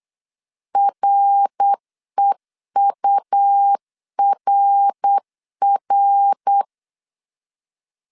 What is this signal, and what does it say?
Morse code, heard as a keyed tone, about 8 wpm, 800 Hz. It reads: REURR
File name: CW / DET / 1 3 1500C S B 8wpm 4141